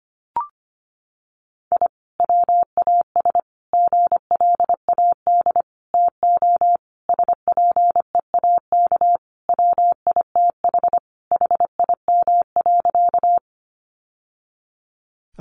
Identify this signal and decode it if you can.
Morse code, heard as a keyed tone, about 25 wpm, 700 Hz. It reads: I WAS GLAD TO SPEAK WITH HIM.
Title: Sentences from Top 1100 Words 25wpm